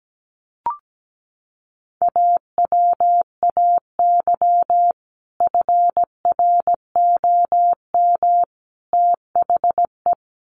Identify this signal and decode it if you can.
Morse code, heard as a keyed tone, about 17 wpm, 700 Hz. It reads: AWAY FROM THE
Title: Sets of 3 Words 17wpm